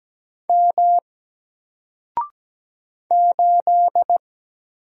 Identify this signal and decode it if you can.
Morse code, heard as a keyed tone, about 17 wpm, 700 Hz. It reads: M 8
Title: Single Character (Letter:Number:Punctuation:Prosign) Alt 17wpm